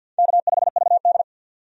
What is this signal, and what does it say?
Morse code, heard as a keyed tone, about 53 wpm, 700 Hz. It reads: K5VD
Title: USA Call Signs - 4 Characters - 50wpm